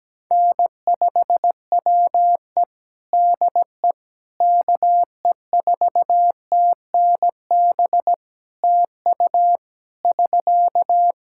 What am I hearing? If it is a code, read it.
Morse code, heard as a keyed tone, about 17 wpm, 700 Hz. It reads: N5WE DE KE4TNB TU <SK>
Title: K1USN SST QSO Practice 17wpm